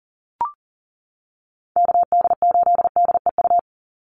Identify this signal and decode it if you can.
Morse code, heard as a keyed tone, about 40 wpm, 700 Hz. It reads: KD8DEU
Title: NA SOTA Chaser Callsigns 40wpm